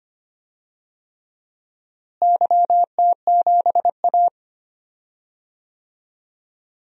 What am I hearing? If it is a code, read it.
Morse code, heard as a keyed tone, about 25 wpm, 700 Hz. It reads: YT7A
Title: DXCC-100 Call Signs - 4 Characters - Repeated 3x - 25wpm